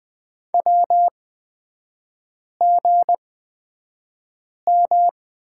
Morse code, heard as a keyed tone, about 20 wpm, spaced as for 5 wpm, 700 Hz. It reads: WGM
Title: 236 Lesson 38e - Letter q - Three Characters 20wpm